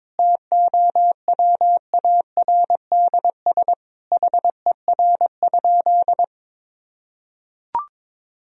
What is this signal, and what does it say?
Morse code, heard as a keyed tone, about 22 wpm, 700 Hz. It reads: TOWARDS HER?
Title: Sentences from Top 1000 Words - Review Part 4 - 22wpm